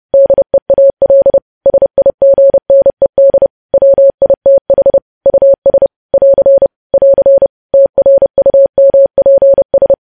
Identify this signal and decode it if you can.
Morse code, heard as a keyed tone, about 30 wpm, 550 Hz. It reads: DEAL SIGNED WITH US <AR> <AR> TRUMPS
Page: News Headlines in Morse Code at 30 WPM